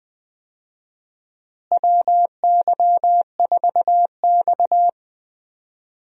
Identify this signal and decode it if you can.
Morse code, heard as a keyed tone, about 20 wpm, 700 Hz. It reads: WY4X
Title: USA Call Signs - 4 Characters - Repeated 2x - 20wpm